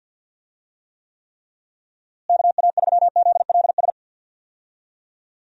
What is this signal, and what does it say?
Morse code, heard as a keyed tone, about 49 wpm, 700 Hz. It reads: KA3ZLS
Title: USA Call Signs - 6 Characters - Repeated 3x - 50wpm